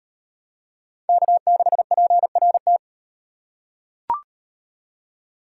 Morse code, heard as a keyed tone, about 38 wpm, 700 Hz. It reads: K6PRT